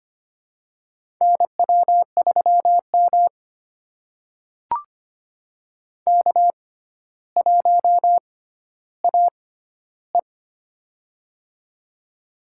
Morse code, heard as a keyed tone, about 25 wpm, spaced as for 13 wpm, 700 Hz. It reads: NW3M K1AE
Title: USA Call Signs - 4 Characters - 6x Character Spacing - 25wpm